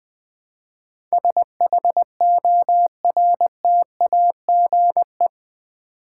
Morse code, heard as a keyed tone, about 20 wpm, 700 Hz. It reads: SHORTAGE
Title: 8 Letter Words - First 3 Letters Given - 20wpm